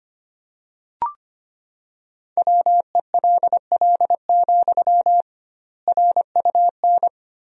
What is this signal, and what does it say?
Morse code, heard as a keyed tone, about 25 wpm, 700 Hz. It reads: WELL, RUN
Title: 2x Word-Spacing - Sentences from Top 300 Words - 25wpm